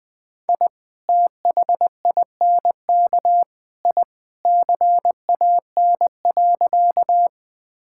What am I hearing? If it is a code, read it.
Morse code, heard as a keyed tone, about 20 wpm, 700 Hz. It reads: I THINK I CAN.
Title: Sentences with 3-Word Phrases - 20wpm